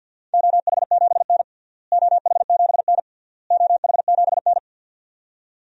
Morse code, heard as a keyed tone, about 48 wpm, 700 Hz. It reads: OH7N OH7N OH7N
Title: DXCC-100 Call Signs - 4 Characters - Repeated 3x - 50wpm